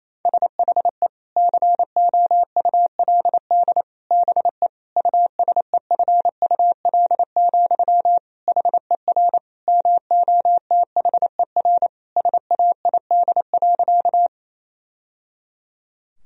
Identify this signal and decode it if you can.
Morse code, heard as a keyed tone, about 28 wpm, 700 Hz. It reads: SHE COULD BE USEFUL, HER MOTHER SAID.
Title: Sentences from Top 1000 Words - Review Part 4 - 28wpm